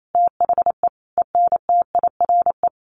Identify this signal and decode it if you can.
Morse code, heard as a keyed tone, about 28 wpm, 700 Hz. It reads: THE ENTIRE
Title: A Sound Of Thunder 28wpm